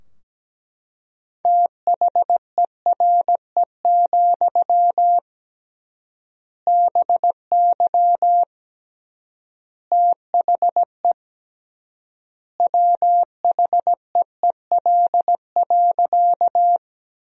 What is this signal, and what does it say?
Morse code, heard as a keyed tone, about 17 wpm, 700 Hz. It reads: THERE, BY THE WHEEL.
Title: 3x Word-Spacing - Sentences from Top 500 Words - 17wpm